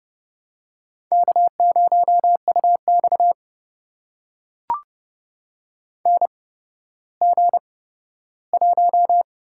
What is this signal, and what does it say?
Morse code, heard as a keyed tone, about 29 wpm, spaced as for 14 wpm, 700 Hz. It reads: K0UX NG1
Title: USA Call Signs - 4 Characters - 8x Character Spacing - 30wpm